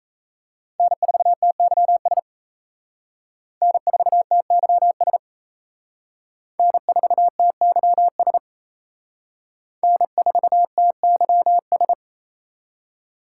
Morse code, heard as a keyed tone, about 35 wpm, 700 Hz. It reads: N4TYS N4TYS N4TYS N4TYS